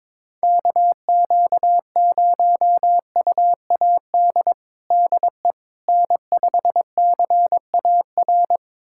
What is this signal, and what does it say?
Morse code, heard as a keyed tone, about 22 wpm, 700 Hz. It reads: KQ0UAD DE N5CAR
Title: USA Callsign DE Callsign 22wpm